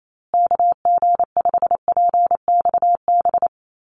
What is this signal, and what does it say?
Morse code, heard as a keyed tone, about 28 wpm, 700 Hz. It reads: KG5PXB
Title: USA Call Signs Repeated 2x 28wpm